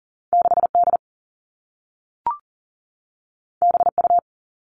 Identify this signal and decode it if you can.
Morse code, heard as a keyed tone, about 42 wpm, 700 Hz. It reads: 6 D B U
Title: Two Challenging Letter-Number Combinations 40wpm